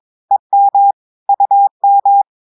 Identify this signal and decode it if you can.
Morse code, heard as a keyed tone, about 22 wpm, 800 Hz. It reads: EM UM
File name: audios / portugais / portugais texte en clair 2 vitesse 22 mots minutes